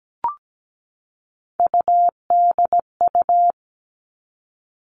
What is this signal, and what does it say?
Morse code, heard as a keyed tone, about 17 wpm, 700 Hz. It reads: UDU
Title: Three Challenging Letter-Number Combinations 17wpm